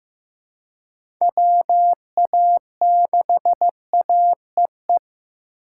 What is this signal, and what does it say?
Morse code, heard as a keyed tone, about 15 wpm, 700 Hz. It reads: WA6AEE
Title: K1USN SST QSO Practice - 15wpm